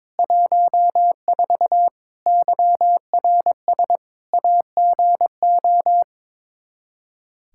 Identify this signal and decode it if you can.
Morse code, heard as a keyed tone, about 21 wpm, 700 Hz. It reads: 14 YRS AGO
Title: Practice QSO - Exchange 3 - More Info - 22wpm